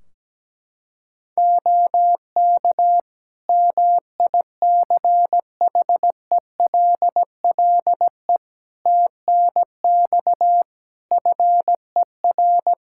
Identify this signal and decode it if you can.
Morse code, heard as a keyed tone, about 17 wpm, 700 Hz. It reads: OK MICHELLE TNX FER
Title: Practice QSO - Exchange 4 - Ending - 17wpm